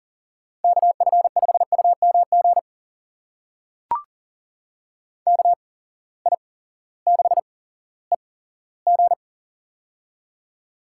Morse code, heard as a keyed tone, about 40 wpm, spaced as for 18 wpm, 700 Hz. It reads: KF5UMG KI6EG
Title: USA Call Signs - 8x Character Spacing - 40wpm